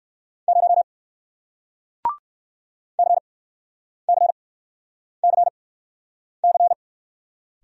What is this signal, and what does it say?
Morse code, heard as a keyed tone, about 60 wpm, spaced as for 9 wpm, 700 Hz. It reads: 0 CCCC